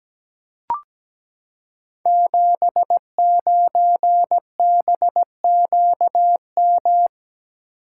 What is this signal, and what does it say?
Morse code, heard as a keyed tone, about 17 wpm, 700 Hz. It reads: 79BQM
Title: Code Groups - 5 Letter:Numbers - 17wpm